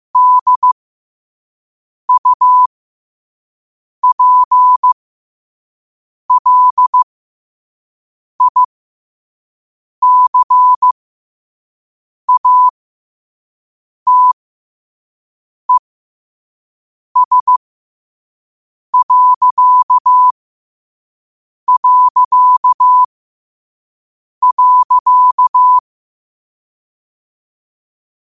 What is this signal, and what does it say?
Morse code, heard as a keyed tone, about 15 wpm, spaced as for 5 wpm, 1000 Hz. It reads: DUPLICATES...
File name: qotdfiles / 5wpm